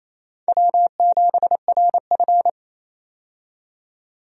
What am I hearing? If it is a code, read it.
Morse code, heard as a keyed tone, about 28 wpm, 700 Hz. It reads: W7RF
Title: USA Call Signs - 4 Characters - 28wpm